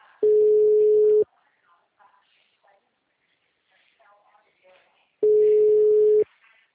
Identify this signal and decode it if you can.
Ringback tone (Europe, 425 Hz), with speech.
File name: out- -2011- - - .18